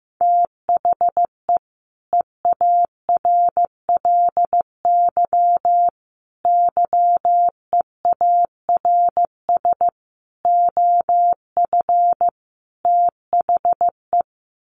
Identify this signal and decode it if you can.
Morse code, heard as a keyed tone, about 15 wpm, 700 Hz. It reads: THE EARLY YEARS OF THE
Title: Sets of 5 Words Encore 15wpm